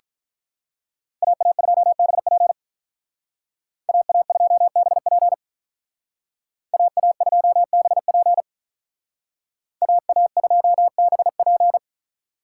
Morse code, heard as a keyed tone, about 45 wpm, 700 Hz. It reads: AA2BP AA2BP AA2BP AA2BP